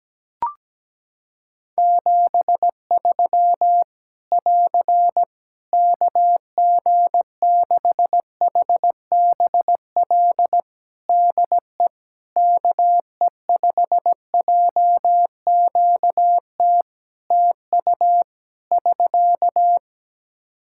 Morse code, heard as a keyed tone, about 17 wpm, 700 Hz. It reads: 73 <AR> KG6HBL DE KE5JQT TU <SK>
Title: Practice QSO - Exchange 4 - Ending - 17wpm